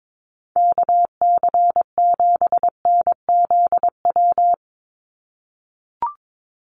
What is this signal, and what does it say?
Morse code, heard as a keyed tone, about 22 wpm, 700 Hz. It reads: KC7NZW
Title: USA Call Signs 22wpm